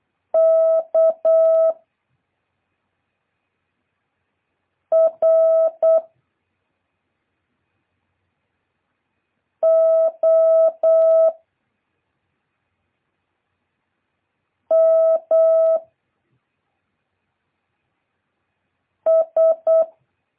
Morse code, heard as a keyed tone, about 8 wpm, 650 Hz. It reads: K R O M S
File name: jh8cbh / 2023cw